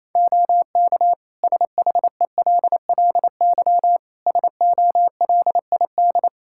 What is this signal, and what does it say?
Morse code, heard as a keyed tone, about 28 wpm, 700 Hz. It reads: OK SHELLY SOLID
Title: Practice QSO - Exchange 3 - More Info - 28wpm